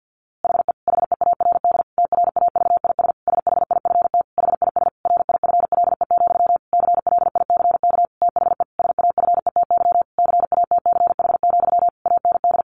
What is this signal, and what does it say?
Morse code, heard as a keyed tone, about 50 wpm, 700 Hz. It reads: HE HEARD TRAVIS SHIFT HIS RIFLE, CLICK THE SAFETY CATCH, AND
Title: A Sound Of Thunder 50wpm